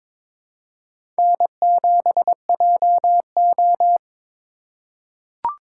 Morse code, heard as a keyed tone, about 22 wpm, 700 Hz. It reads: N7JO
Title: USA Call Signs - 4 Characters - 2x Character Spacing - 22wpm